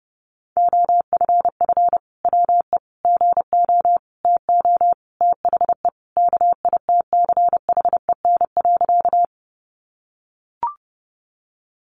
Morse code, heard as a keyed tone, about 30 wpm, 700 Hz. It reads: OFF WE GO TO THE KITCHEN.
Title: Sentences from Top 1400 Words 30wpm